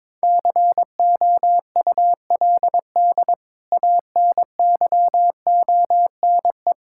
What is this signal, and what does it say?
Morse code, heard as a keyed tone, about 21 wpm, 700 Hz. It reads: COULD ANYONE
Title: Sentences from Top 1000 Words - Review Part 5 - 22wpm